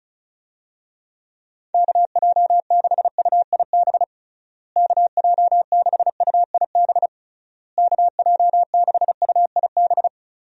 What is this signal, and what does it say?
Morse code, heard as a keyed tone, about 34 wpm, 700 Hz. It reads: KJ6UIB KJ6UIB KJ6UIB
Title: USA Call Signs - 6 Characters - Repeated 3x - 35wpm